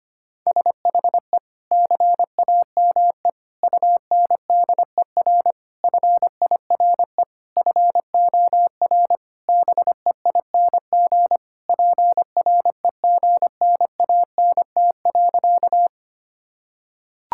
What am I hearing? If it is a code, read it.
Morse code, heard as a keyed tone, about 25 wpm, 700 Hz. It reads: SHE CAME UNDER FIRE FOR BEING PREGNANT.
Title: Sentences from Top 1000 Words - Review Part 6 - 25wpm